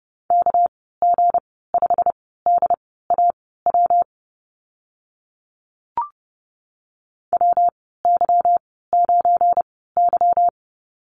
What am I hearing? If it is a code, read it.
Morse code, heard as a keyed tone, about 30 wpm, spaced as for 17 wpm, 700 Hz. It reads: KG5DAW WY9Y
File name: Lesson 37g - Letter x - Call Signs 30wpm